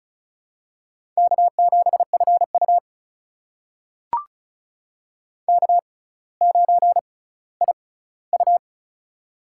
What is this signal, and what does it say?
Morse code, heard as a keyed tone, about 34 wpm, spaced as for 18 wpm, 700 Hz. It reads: K7FU K9IU
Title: USA Call Signs - 4 Characters - 6x Character Spacing - 35wpm